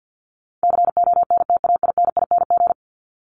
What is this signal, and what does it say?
Morse code, heard as a keyed tone, about 49 wpm, 700 Hz. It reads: CONTAINING
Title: ING Suffix - Top 100 Words with Base Words 50wpm